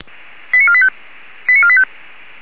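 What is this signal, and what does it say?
Five-tone selective call (ZVEI), heard as quick runs of five tones, tones of about 70 ms. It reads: 89485 89485
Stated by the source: ZVEI five-tone call received via SDR.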